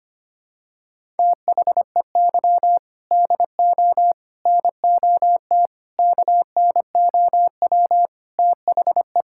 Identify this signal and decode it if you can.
Morse code, heard as a keyed tone, about 25 wpm, 700 Hz. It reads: THEY DO NOT KNOW THE
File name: practice / 25wpm / Sentences with 4-Word Phrases - 25wpm